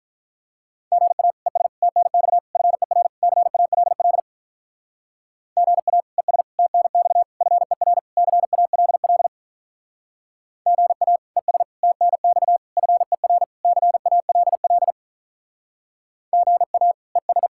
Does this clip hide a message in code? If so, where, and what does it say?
Yes: Morse code from 0.9 to 17.5 s, heard as a keyed tone, about 45 wpm, 700 Hz. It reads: GA ES TNX FER CALL GA ES TNX FER CALL GA ES TNX FER CALL GA ES